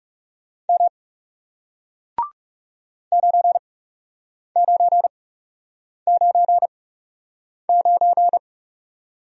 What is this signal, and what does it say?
Morse code, heard as a keyed tone, about 37 wpm, spaced as for 8 wpm, 700 Hz. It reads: M 9999
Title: Speed-Racing Single Letter-Number 30wpm